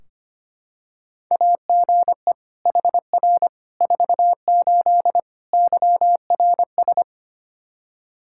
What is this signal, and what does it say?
Morse code, heard as a keyed tone, about 25 wpm, 700 Hz. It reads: AGE HR 48 YRS
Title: Practice QSO - Exchange 3 - More Info - 25wpm